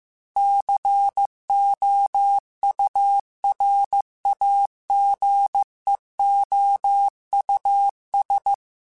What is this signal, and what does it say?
Morse code, heard as a keyed tone, about 15 wpm, 800 Hz. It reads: COURAGEOUS